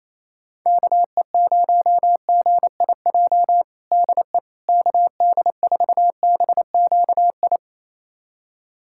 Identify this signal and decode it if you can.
Morse code, heard as a keyed tone, about 28 wpm, 700 Hz. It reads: KE0GIJ DE KD4BQI
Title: USA Callsign DE Callsign - 6 Characters - 28wpm